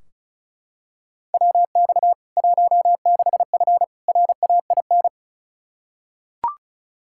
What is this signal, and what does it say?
Morse code, heard as a keyed tone, about 34 wpm, 700 Hz. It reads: WX 16F RAIN
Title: K1USN SST QSO Practice 35wpm